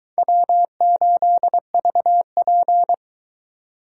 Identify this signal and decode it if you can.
Morse code, heard as a keyed tone, about 22 wpm, 700 Hz. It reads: W8VP